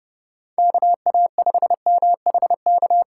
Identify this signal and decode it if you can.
Morse code, heard as a keyed tone, about 30 wpm, 700 Hz. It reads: KA5MHK
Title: USA Call Signs - 6 Characters - 3x Character Spacing - 30wpm